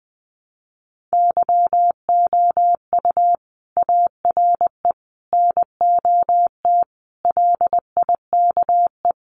Morse code, heard as a keyed tone, about 20 wpm, 700 Hz. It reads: YOU ARE NOT LIKE
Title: Sentences from Top 600 Words 20wpm